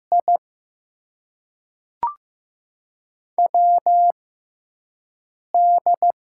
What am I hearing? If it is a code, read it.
Morse code, heard as a keyed tone, about 15 wpm, spaced as for 5 wpm, 700 Hz. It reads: I WD
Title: USA Call Signs - 6 Characters - 6x Character Spacing - 15wpm